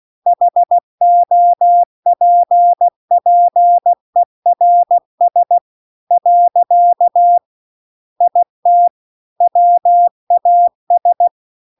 Morse code, heard as a keyed tone, about 16 wpm, 700 Hz. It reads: HOPPERS . IT WAS